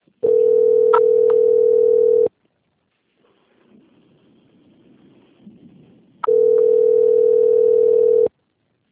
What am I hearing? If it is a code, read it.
Ringback tone (North America), 440 + 480 Hz.